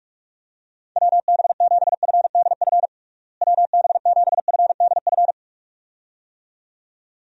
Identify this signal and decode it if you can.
Morse code, heard as a keyed tone, about 46 wpm, 700 Hz. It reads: WB7FDF WB7FDF
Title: USA Call Signs - 6 Characters - Repeated 2x - 45wpm